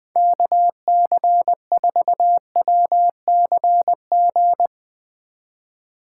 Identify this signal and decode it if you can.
Morse code, heard as a keyed tone, about 20 wpm, 700 Hz. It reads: KC4WCG
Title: Speed Racing - USA 2x3 Call Signs 17wpm